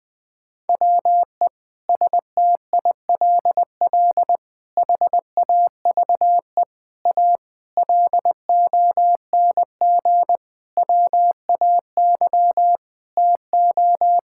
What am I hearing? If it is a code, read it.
Morse code, heard as a keyed tone, about 20 wpm, 700 Hz. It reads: WE STILL HAVE A LONG WAY TO
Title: Sentences with 6-Word Phrases - Spoken Beginning - 20wpm